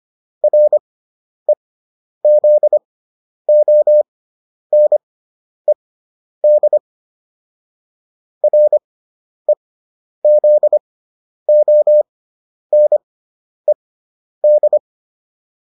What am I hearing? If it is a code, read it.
Morse code, heard as a keyed tone, about 25 wpm, spaced as for 10 wpm, 600 Hz. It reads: REZONED REZONED